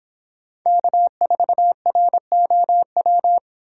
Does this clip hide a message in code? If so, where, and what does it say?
Yes: Morse code from 0.7 to 3.4 s, heard as a keyed tone, about 30 wpm, 700 Hz. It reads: K4ROW